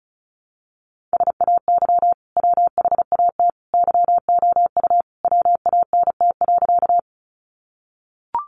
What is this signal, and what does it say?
Morse code, heard as a keyed tone, about 34 wpm, 700 Hz. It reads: SAY WHAT YOU WANT.
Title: Sentences from Top 200 Words 35wpm